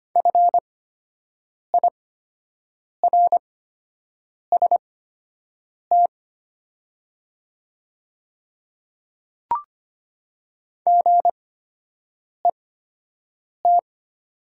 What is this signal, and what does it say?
Morse code, heard as a keyed tone, about 25 wpm, spaced as for 7 wpm, 700 Hz. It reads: FIRST GET
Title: 8x Character Spacing - Top 200 Words - 25wpm